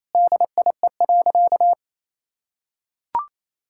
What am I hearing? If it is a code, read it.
Morse code, heard as a keyed tone, about 27 wpm, 700 Hz. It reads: DIE.
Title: Sentences from Top 1600 Words 28wpm